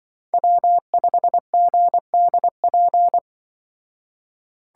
Morse code, heard as a keyed tone, about 24 wpm, 700 Hz. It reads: W5GDP
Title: Speed Racing - USA 1x3 Call Signs 20wpm